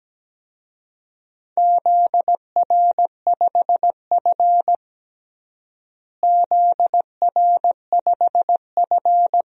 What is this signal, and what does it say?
Morse code, heard as a keyed tone, about 17 wpm, 700 Hz. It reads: ZR5F ZR5F
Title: DXCC-100 Call Signs - 4 Characters - Repeated 2x - 17wpm